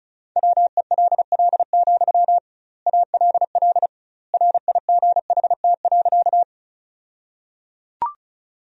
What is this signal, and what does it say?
Morse code, heard as a keyed tone, about 34 wpm, 700 Hz. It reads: WELL, ALL RIGHT.
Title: 2x Word-Spacing - Sentences from Top 200 Words - 35wpm